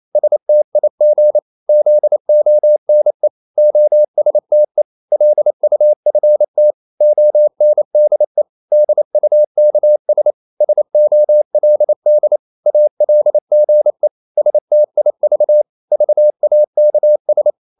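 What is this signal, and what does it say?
Morse code, heard as a keyed tone, about 28 wpm, 600 Hz. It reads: STIG ZONE OSTE LUFT ONDE DUKS SOLD ALGE STIV VAKS
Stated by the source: Danish short words sent in Morse at 28 wpm, from CW med Gnister.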